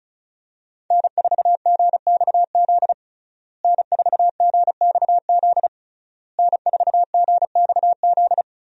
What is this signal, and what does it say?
Morse code, heard as a keyed tone, about 36 wpm, 700 Hz. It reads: N4GXZ N4GXZ N4GXZ
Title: USA Call Signs - 5 Characters - Repeated 3x - 35wpm